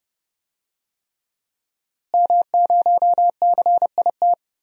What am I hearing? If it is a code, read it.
Morse code, heard as a keyed tone, about 31 wpm, 700 Hz. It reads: M0CIT
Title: DXCC-100 Call Signs - 5 Characters - Repeated 3x - 30wpm